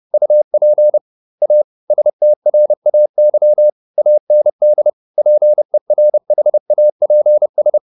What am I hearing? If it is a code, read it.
Morse code, heard as a keyed tone, about 30 wpm, 600 Hz. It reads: UP A STRAY AND PERHAPS
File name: the old man and the sea / the old man and the sea 1 30wpm